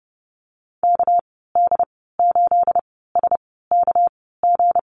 Morse code, heard as a keyed tone, about 30 wpm, spaced as for 17 wpm, 700 Hz. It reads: KD8SKG
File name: Lesson 37g - Letter x - Call Signs 30wpm